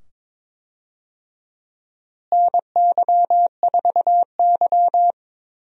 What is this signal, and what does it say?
Morse code, heard as a keyed tone, about 22 wpm, 700 Hz. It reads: NY4Y
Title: USA Call Signs - 4 Characters - Repeated 2x - 22wpm